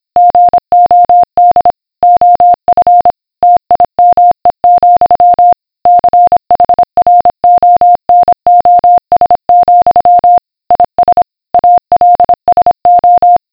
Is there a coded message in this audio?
Yes: Morse code from 0.2 to 13.4 s, heard as a keyed tone, about 26 wpm, 710 Hz. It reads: GOD OF TIME, CHRONOS, IS ALSO